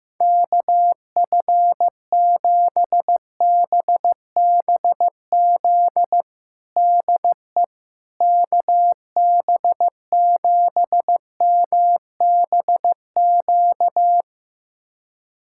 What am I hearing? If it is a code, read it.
Morse code, heard as a keyed tone, about 16 wpm, 700 Hz. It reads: KF7BBZ DE KB7MBQ
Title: USA Callsign DE Callsign - 6 Characters - 15wpm